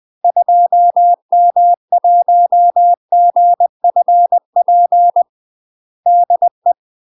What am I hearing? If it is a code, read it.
Morse code, heard as a keyed tone, about 20 wpm, 700 Hz. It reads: 2M1GFP DE
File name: UK MASTER QSOs TRAINING/20WPM